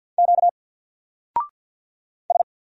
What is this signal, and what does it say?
Morse code, heard as a keyed tone, about 50 wpm, 700 Hz. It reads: Q S
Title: Single Letter-Number 50wpm